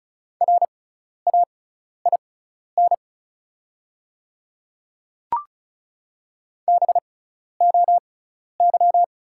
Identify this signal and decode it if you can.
Morse code, heard as a keyed tone, about 36 wpm, spaced as for 12 wpm, 700 Hz. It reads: RAIN BOY
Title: 6x Character Spacing - Top 300 Words - 35wpm